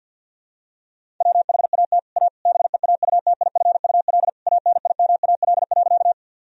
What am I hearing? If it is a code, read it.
Morse code, heard as a keyed tone, about 52 wpm, 700 Hz. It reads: WHAT A BEAUTIFUL ANIMAL.
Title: Sentences from Top 1000 Words - Review Part 1 - 50wpm